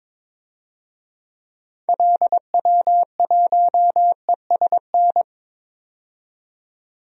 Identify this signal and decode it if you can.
Morse code, heard as a keyed tone, about 22 wpm, 700 Hz. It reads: LW1ESN
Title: DXCC-100 Call Signs - 6 Characters - 22wpm